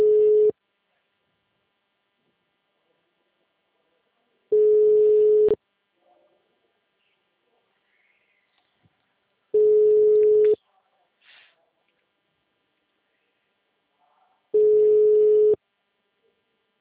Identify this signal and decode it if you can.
Ringback tone (Europe, 425 Hz).